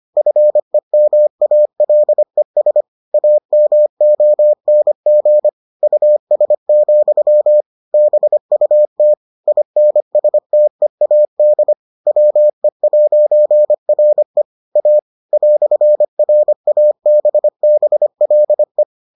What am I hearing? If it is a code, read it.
Morse code, heard as a keyed tone, about 25 wpm, 600 Hz. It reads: FEMALES AMONG US, BUT INSTEAD WE'RE A "RABBLE